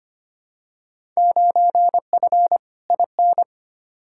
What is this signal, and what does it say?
Morse code, heard as a keyed tone, about 25 wpm, 700 Hz. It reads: 9F IN
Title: Field Day Exchanges Encore - 25wpm